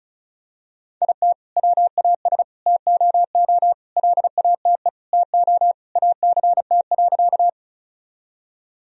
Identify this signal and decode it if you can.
Morse code, heard as a keyed tone, about 36 wpm, 700 Hz. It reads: IT WAS TOO LATE TO ACT.
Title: Sentences from Top 1000 Words - Review Part 1 - 35wpm